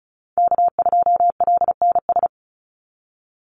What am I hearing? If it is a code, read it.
Morse code, heard as a keyed tone, about 34 wpm, 700 Hz. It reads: K2LNS
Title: Call Signs - 35wpm